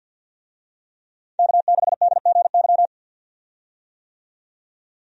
Morse code, heard as a keyed tone, about 50 wpm, 700 Hz. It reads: K6DGY
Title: USA Call Signs - 5 Characters - 50wpm